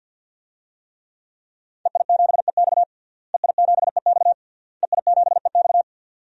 Morse code, heard as a keyed tone, about 51 wpm, 700 Hz. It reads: EI7EX EI7EX EI7EX
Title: DXCC-100 Call Signs - 5 Characters - Repeated 3x - 50wpm